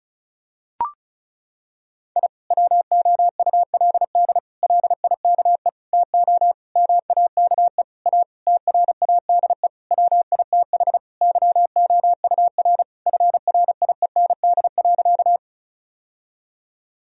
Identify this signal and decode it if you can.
Morse code, heard as a keyed tone, about 34 wpm, 700 Hz. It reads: I WOULD LIKE TO MAKE A TRADE WITH YOUR FRIEND.
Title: Sentences from Top 1000 Words - Review Part 4 - 35wpm